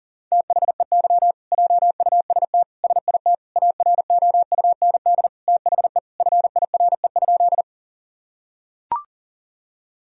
Morse code, heard as a keyed tone, about 40 wpm, 700 Hz. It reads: THEY JUST SIT AROUND THE FIRE?
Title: Sentences from Top 1000 Words - Review Part 3 - 40wpm